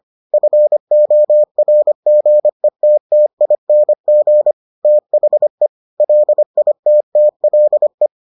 Morse code, heard as a keyed tone, about 25 wpm, 600 Hz. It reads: FORGETTING THE LITTLE